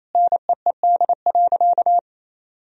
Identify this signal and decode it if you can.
Morse code, heard as a keyed tone, about 28 wpm, 700 Hz. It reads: NEED.
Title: Sentences from Top 1000 Words - Review Part 3 - 28wpm